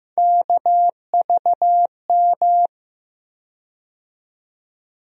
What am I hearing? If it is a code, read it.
Morse code, heard as a keyed tone, about 15 wpm, 700 Hz. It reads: KVM
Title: USA Call Sign Postfixes - 3 Letters 15wpm